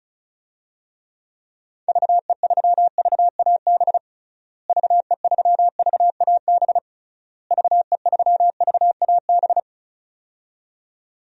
Morse code, heard as a keyed tone, about 36 wpm, 700 Hz. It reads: VE3VAB VE3VAB VE3VAB
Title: DXCC-100 Call Signs - 6 Characters - Repeated 3x - 35wpm